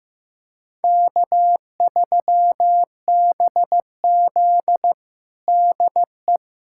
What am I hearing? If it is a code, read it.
Morse code, heard as a keyed tone, about 15 wpm, 700 Hz. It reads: K3BZ DE
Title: USA Callsign DE Callsign - 4 Characters - 15wpm